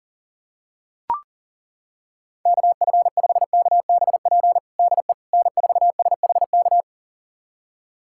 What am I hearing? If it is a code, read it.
Morse code, heard as a keyed tone, about 40 wpm, 700 Hz. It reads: KF5KBP DE N4SHK
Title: USA Callsign DE Callsign 40wpm